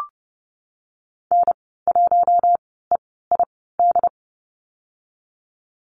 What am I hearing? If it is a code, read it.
Morse code, heard as a keyed tone, about 30 wpm, spaced as for 17 wpm, 700 Hz. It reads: N1EID
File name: Lesson 14g - Number 2 - Call Signs 30wpm